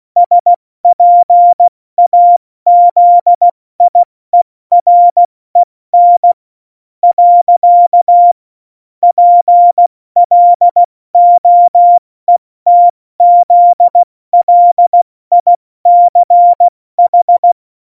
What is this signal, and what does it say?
Morse code, heard as a keyed tone, about 16 wpm, spaced as for 14 wpm, 700 Hz. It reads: SPAZIEREN . PLOETZLICH
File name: Morsekurs / Witze / Witz 26. Juni 2018, 70 F80